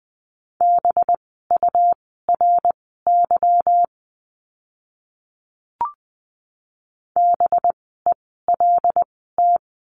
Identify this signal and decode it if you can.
Morse code, heard as a keyed tone, about 20 wpm, spaced as for 14 wpm, 700 Hz. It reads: BURY BELT
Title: 185 Lesson 30f - Letter b - Words 20wpm